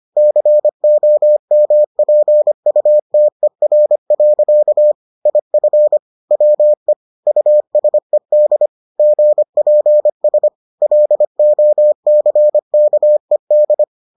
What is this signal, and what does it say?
Morse code, heard as a keyed tone, about 25 wpm, 600 Hz. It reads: COMPUTER. IF WE USED GPS LOCKED